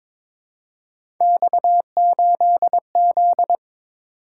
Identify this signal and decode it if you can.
Morse code, heard as a keyed tone, about 22 wpm, 700 Hz. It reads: X8Z
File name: practice / 22wpm / Three Letter-Number Combinations - 22wpm